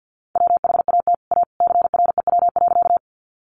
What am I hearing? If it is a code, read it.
Morse code, heard as a keyed tone, about 50 wpm, 700 Hz. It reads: WHAT A CREW.
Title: Sentences from Top 1500 Words 50wpm